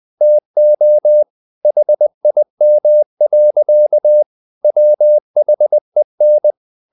Morse code, heard as a keyed tone, about 20 wpm, 600 Hz. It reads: TO HIM. WHEN
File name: the old man and the sea part 2 20wpm